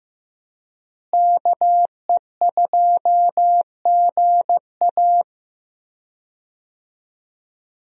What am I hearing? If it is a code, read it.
Morse code, heard as a keyed tone, about 15 wpm, 700 Hz. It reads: KE2GA
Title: USA Call Signs - 5 Characters - Repeated 3x - 15wpm